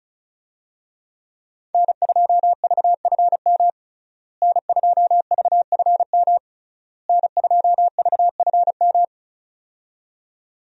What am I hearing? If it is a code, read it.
Morse code, heard as a keyed tone, about 35 wpm, 700 Hz. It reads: N2VFM N2VFM N2VFM
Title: USA Call Signs - 5 Characters - Repeated 3x - 35wpm